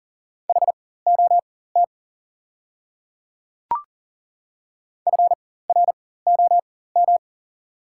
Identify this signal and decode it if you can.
Morse code, heard as a keyed tone, about 40 wpm, spaced as for 19 wpm, 700 Hz. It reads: HOT FROM
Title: 4x Character Spacing - Top 100 Words - 40wpm